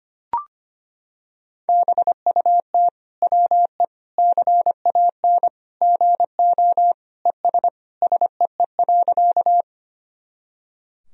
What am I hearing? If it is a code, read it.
Morse code, heard as a keyed tone, about 25 wpm, 700 Hz. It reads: BUT WE CAN GO ES SEE.
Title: Sentences from Top 100 Words and Abbreviations 25wpm